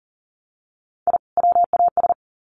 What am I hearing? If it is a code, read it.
Morse code, heard as a keyed tone, about 40 wpm, 700 Hz. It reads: I WAS
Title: Sets of 2 Words 40wpm